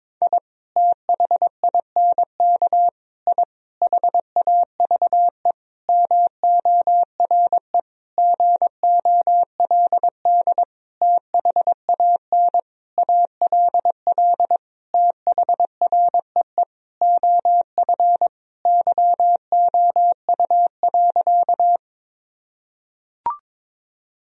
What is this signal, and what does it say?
Morse code, heard as a keyed tone, about 22 wpm, 700 Hz. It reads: I THINK I HAVE MORE GOLD THAN ALL THREE OF YOU.
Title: Sentences from Top 1000 Words - Review Part 3 - 22wpm